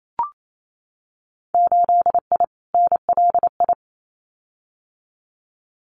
Morse code, heard as a keyed tone, about 28 wpm, 690 Hz. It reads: 8I NLI
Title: Winter Field Day Exchanges Encore 28wpm